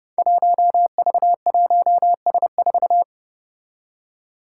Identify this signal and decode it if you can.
Morse code, heard as a keyed tone, about 30 wpm, 700 Hz. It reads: 1V1S4
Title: Five Challenging Letter-Number Combinations 30wpm